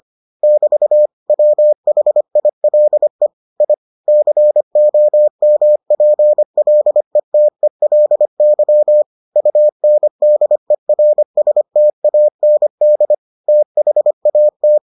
Morse code, heard as a keyed tone, about 25 wpm, 600 Hz. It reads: <BT> WHILE I COMPLETELY UNDERSTAND THAT